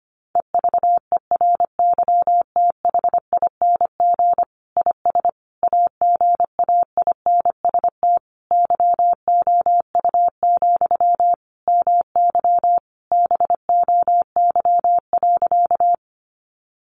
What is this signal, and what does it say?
Morse code, heard as a keyed tone, about 25 wpm, 700 Hz. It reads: EVERYTHING IS AGAINST YOU, MY BOY.
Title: Sentences from Top 700 Words 25wpm